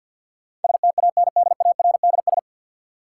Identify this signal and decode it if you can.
Morse code, heard as a keyed tone, about 51 wpm, 700 Hz. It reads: STANDARDS
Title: Sentences from Top 1000 Words - Review Part 6 - New Word Drill - 50wpm